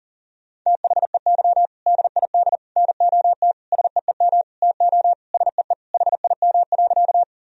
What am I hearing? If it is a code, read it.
Morse code, heard as a keyed tone, about 40 wpm, 700 Hz. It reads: THEY DID NOT SEEM TO SEE HIM.
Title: Sentences from Top 1000 Words - Review Part 1 - 40wpm